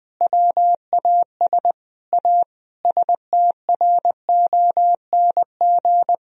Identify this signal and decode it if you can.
Morse code, heard as a keyed tone, about 21 wpm, 700 Hz. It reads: WAS A STRONG
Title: Sentences from Top 1000 Words - Review Part 3 - 20wpm